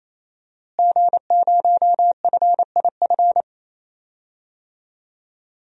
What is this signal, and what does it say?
Morse code, heard as a keyed tone, about 28 wpm, 700 Hz. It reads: G0FIF
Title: DXCC-100 Call Signs - 5 Characters - Repeated 3x - 28wpm